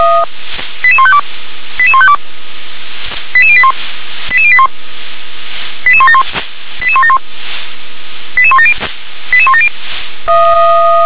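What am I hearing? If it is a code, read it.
Five-tone selective call (ZVEI), heard as quick runs of five tones, tones of about 70 ms. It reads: 88152 88152 88081 88081 88171 88171 88170 88170
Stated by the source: ZVEI five-tone selective call 88126 received on an SDR.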